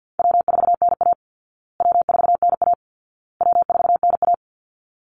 Morse code, heard as a keyed tone, about 48 wpm, 700 Hz. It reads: W4NA W4NA W4NA
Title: Easy Call Signs Repeated 3x - 50wpm